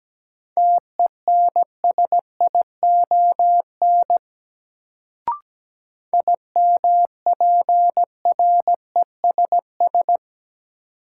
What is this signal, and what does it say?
Morse code, heard as a keyed tone, about 17 wpm, 700 Hz. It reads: TENSION IMPRESS
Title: Sentences from Top 1600 Words - New Word Drill - 17wpm